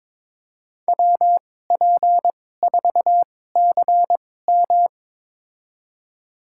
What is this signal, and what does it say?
Morse code, heard as a keyed tone, about 21 wpm, spaced as for 16 wpm, 700 Hz. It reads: WP4CM
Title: USA Call Signs - 5 Characters - 2x Character Spacing - 22wpm